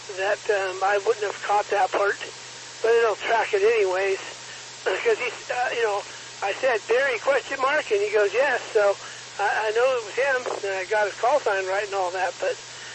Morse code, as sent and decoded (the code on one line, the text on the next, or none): none